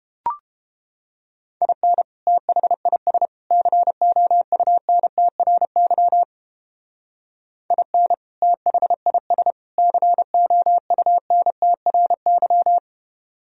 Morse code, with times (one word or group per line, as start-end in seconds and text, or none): 1.6-2.0 IN
2.3-3.3 THIS
3.5-6.2 COUNTRY
7.7-8.1 IN
8.4-9.5 THIS
9.8-12.8 COUNTRY